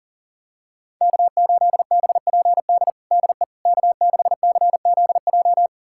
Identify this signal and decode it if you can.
Morse code, heard as a keyed tone, about 40 wpm, 700 Hz. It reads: K8BPD DE K6CZJ
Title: USA Callsign DE Callsign - 5 Characters - 40wpm